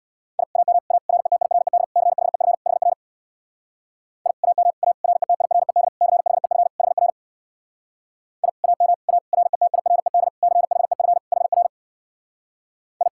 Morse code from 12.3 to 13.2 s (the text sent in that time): I